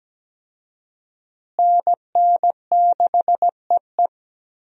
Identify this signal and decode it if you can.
Morse code, heard as a keyed tone, about 17 wpm, 700 Hz. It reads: NN6EE
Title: USA Call Signs - 5 Characters - Repeated 2x - 17wpm